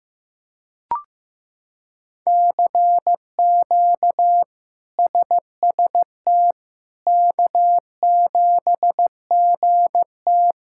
Morse code, heard as a keyed tone, about 16 wpm, 700 Hz. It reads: CQ SST K7GT
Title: K1USN SST QSO Practice - 15wpm